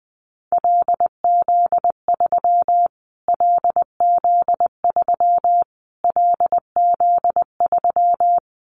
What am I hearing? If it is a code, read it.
Morse code, heard as a keyed tone, about 20 wpm, 700 Hz. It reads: LZ3 LZ3 LZ3
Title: Callsign Prefixes for Top 25 DXCC - Repeated 3x - 20wpm